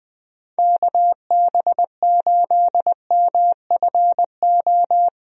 Morse code, heard as a keyed tone, about 20 wpm, 700 Hz. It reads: KB8MFO
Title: USA Call Signs - 6 Characters - Repeated 2x - 20wpm